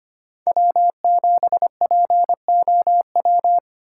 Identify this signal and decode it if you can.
Morse code, heard as a keyed tone, about 25 wpm, 700 Hz. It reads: W7POW
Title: USA Call Signs - 5 Characters - 3x Character Spacing - 25wpm